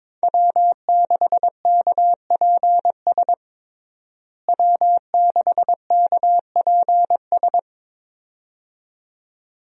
Morse code, heard as a keyed tone, about 22 wpm, 700 Hz. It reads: W6KPS W6KPS
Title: USA Call Signs - 5 Characters - Repeated 2x - 22wpm